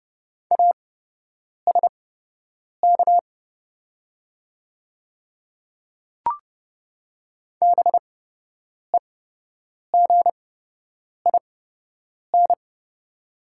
Morse code, heard as a keyed tone, about 31 wpm, spaced as for 8 wpm, 700 Hz. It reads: ASK BEGIN
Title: 8x Character Spacing - Top 300 Words - 30wpm